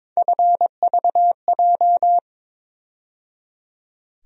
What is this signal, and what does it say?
Morse code, heard as a keyed tone, about 22 wpm, 700 Hz. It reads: FVJ
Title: USA Call Sign Postfixes - 3 Letters 22wpm